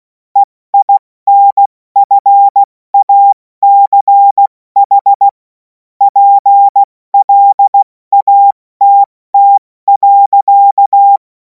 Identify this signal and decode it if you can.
Morse code, heard as a keyed tone, about 16 wpm, spaced as for 14 wpm, 800 Hz. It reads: EINFACH PLATT.
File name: Morsekurs / Witze / Witz 10. März 2026 70 F80